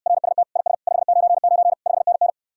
Morse code, heard as a keyed tone, about 66 wpm, 700 Hz. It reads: RST IS 599 5NN